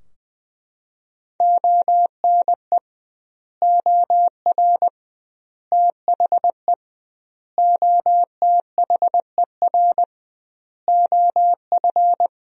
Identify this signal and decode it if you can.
Morse code, heard as a keyed tone, about 21 wpm, 700 Hz. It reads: ONE OR THE OTHER OF
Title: 2x Word-Spacing - Sentences from Top 400 Words - 20wpm